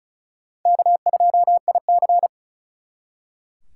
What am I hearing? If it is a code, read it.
Morse code, heard as a keyed tone, about 48 wpm, spaced as for 42 wpm, 700 Hz. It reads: K2IC
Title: Speed Racing - USA Call Signs 35wpm